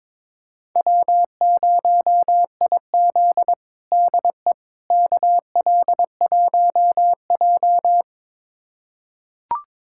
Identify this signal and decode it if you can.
Morse code, heard as a keyed tone, about 22 wpm, 700 Hz. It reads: W0IZ DE KL1J
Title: USA Callsign DE Callsign - 4 Characters - 22wpm